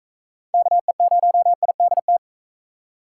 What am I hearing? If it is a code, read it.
Morse code, heard as a keyed tone, about 43 wpm, 700 Hz. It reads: KE0IDT